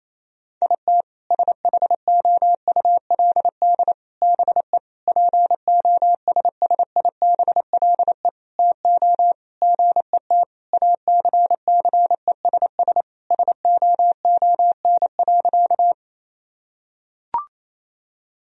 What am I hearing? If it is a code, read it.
Morse code, heard as a keyed tone, about 27 wpm, 700 Hz. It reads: IT SHOULD BE POSSIBLE TO GET ACCESS SOON.
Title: Sentences from Top 1000 Words - Review Part 4 - 28wpm